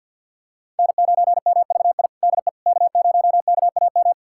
Morse code, heard as a keyed tone, about 49 wpm, 700 Hz. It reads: N9MVI DE K0KAM
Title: USA Callsign DE Callsign - 5 Characters - 50wpm